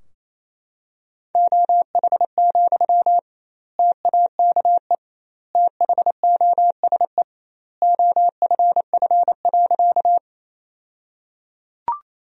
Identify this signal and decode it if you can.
Morse code, heard as a keyed tone, about 28 wpm, 700 Hz. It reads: OH, TAKE THOSE OFF.
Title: 2x Word-Spacing - Sentences from Top 500 Words - 28wpm